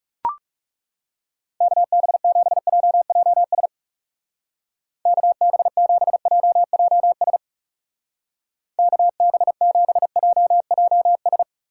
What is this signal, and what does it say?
Morse code, heard as a keyed tone, about 37 wpm, 700 Hz. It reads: KB7JJS KB7JJS KB7JJS
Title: Speed Racing - USA 2x3 Call Signs 30wpm